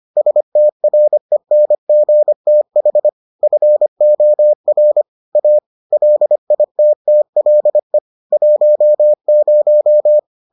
Morse code, heard as a keyed tone, about 25 wpm, 600 Hz. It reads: STRENGTH FOR A LITTLE 10